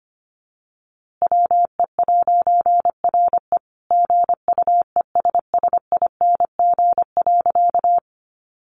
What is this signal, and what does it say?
Morse code, heard as a keyed tone, about 25 wpm, 700 Hz. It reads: WE'RE GUESSING.
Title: A Sound Of Thunder 25wpm